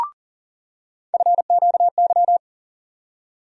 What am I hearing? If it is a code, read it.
Morse code, heard as a keyed tone, about 40 wpm, 700 Hz. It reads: FQY
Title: USA Call Sign Postfixes - 3 Letters 40wpm